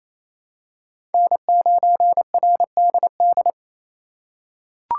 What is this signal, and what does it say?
Morse code, heard as a keyed tone, about 28 wpm, 700 Hz. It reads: N9RDD
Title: USA Call Signs - 5 Characters - 6x Character Spacing - 28wpm